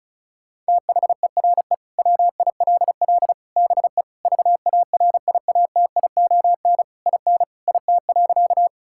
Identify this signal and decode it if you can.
Morse code, heard as a keyed tone, about 36 wpm, 700 Hz. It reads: THERE WILL BE VARIATION IN IT.
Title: Sentences from Top 1600 Words 35wpm